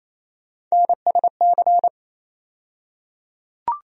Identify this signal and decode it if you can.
Morse code, heard as a keyed tone, about 28 wpm, 700 Hz. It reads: NSC
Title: USA Call Sign Postfixes - 3 Letters 28wpm